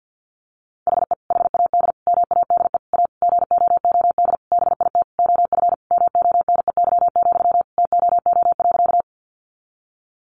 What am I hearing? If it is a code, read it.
Morse code, heard as a keyed tone, about 50 wpm, 700 Hz. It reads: HE HAD MADE A GOOD BIT OF MONEY, TOO.
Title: Sentences from Top 900 Words 50wpm